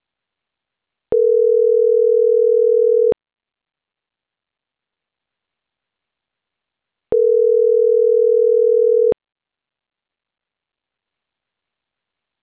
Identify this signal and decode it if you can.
Ringback tone (North America), 440 + 480 Hz.